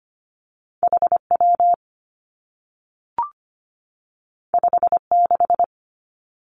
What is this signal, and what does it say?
Morse code, heard as a keyed tone, about 25 wpm, 700 Hz. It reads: H W 5 6
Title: Two Challenging Letter-Number Combinations 25wpm